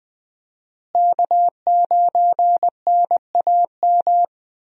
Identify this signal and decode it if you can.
Morse code, heard as a keyed tone, about 20 wpm, 700 Hz. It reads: K9NAM
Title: USA Call Signs - 5 Characters - Repeated 3x - 20wpm